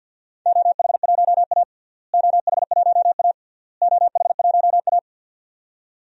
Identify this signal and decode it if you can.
Morse code, heard as a keyed tone, about 47 wpm, 700 Hz. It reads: OH1A OH1A OH1A
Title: DXCC-100 Call Signs - 4 Characters - Repeated 3x - 50wpm